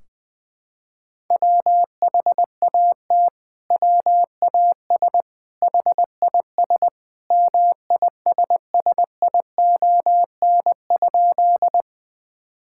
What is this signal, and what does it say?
Morse code, heard as a keyed tone, about 21 wpm, 700 Hz. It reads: WHAT WAS HIS MISSION?
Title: Sentences from Top 1600 Words 20wpm